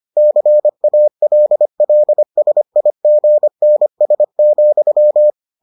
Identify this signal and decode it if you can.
Morse code, heard as a keyed tone, about 25 wpm, 600 Hz. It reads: CALLSIGNS,